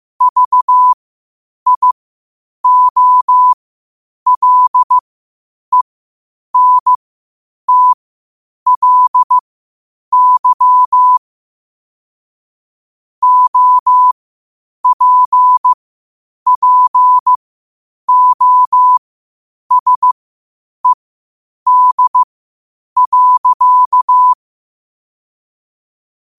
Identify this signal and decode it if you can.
Morse code, heard as a keyed tone, about 15 wpm, spaced as for 8 wpm, 1000 Hz. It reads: VIOLENTLY OPPOSED.
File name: qotdfiles / 8wpm